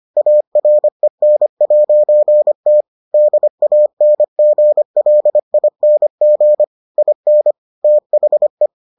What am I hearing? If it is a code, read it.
Morse code, heard as a keyed tone, about 25 wpm, 600 Hz. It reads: AREN'T DANGLING IN THE